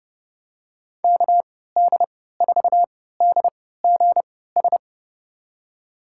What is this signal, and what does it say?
Morse code, heard as a keyed tone, about 31 wpm, spaced as for 17 wpm, 700 Hz. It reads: KD4DGS
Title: USA Call Signs - 6 Characters - 3x Character Spacing - 30wpm